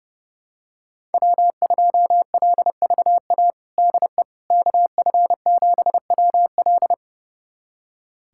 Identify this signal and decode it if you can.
Morse code, heard as a keyed tone, about 30 wpm, 700 Hz. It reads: W2LVA DE KF7WL
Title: USA Callsign DE Callsign - 5 Characters - 30wpm